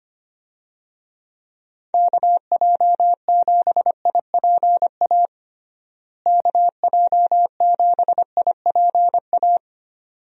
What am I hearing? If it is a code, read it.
Morse code, heard as a keyed tone, about 25 wpm, 700 Hz. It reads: KJ7IPA KJ7IPA
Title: USA Call Signs - 6 Characters - Repeated 2x - 25wpm